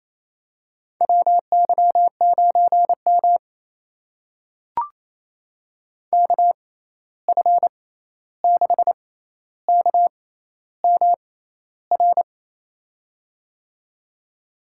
Morse code, heard as a keyed tone, about 28 wpm, spaced as for 13 wpm, 700 Hz. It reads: WY9M KF6KMR
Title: USA Call Signs - 6x Character Spacing - 28wpm